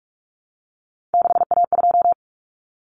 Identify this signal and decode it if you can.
Morse code, heard as a keyed tone, about 44 wpm, 700 Hz. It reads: 6A2
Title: Common Cut Numbers - 3-Digit 45wpm